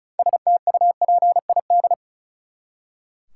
Morse code, heard as a keyed tone, about 34 wpm, 700 Hz. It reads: STUPID
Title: Six Letter Words - 35wpm